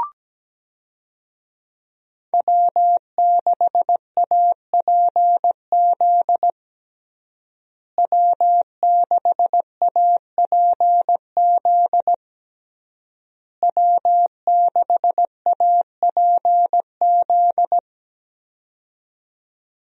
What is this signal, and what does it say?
Morse code, heard as a keyed tone, about 17 wpm, 700 Hz. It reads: W6APZ W6APZ W6APZ
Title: USA Call Signs - 5 Characters - Repeated 3x - 17wpm